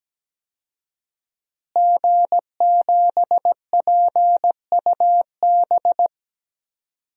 Morse code, heard as a keyed tone, about 17 wpm, 700 Hz. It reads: G7PUB
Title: DXCC-100 Call Signs - 5 Characters - Repeated 3x - 17wpm